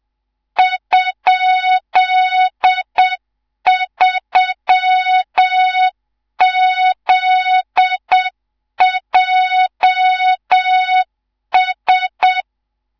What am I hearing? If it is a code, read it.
Morse code, heard as a keyed tone, about 7 wpm, 3700 Hz, with noise. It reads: ?3ZJS
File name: Morze-HA2MN / hang / 19 GYAKORLAT